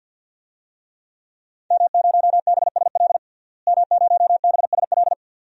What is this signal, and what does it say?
Morse code, heard as a keyed tone, about 53 wpm, 700 Hz. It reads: M0BSL M0BSL
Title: DXCC-100 Call Signs - 5 Characters - Repeated 2x - 50wpm